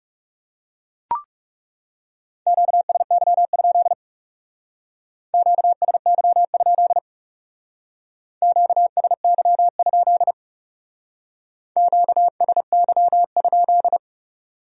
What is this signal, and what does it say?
Morse code, heard as a keyed tone, about 36 wpm, 700 Hz. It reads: QSY? QSY? QSY? QSY?
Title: Speed-Racing Q-Codes - 30wpm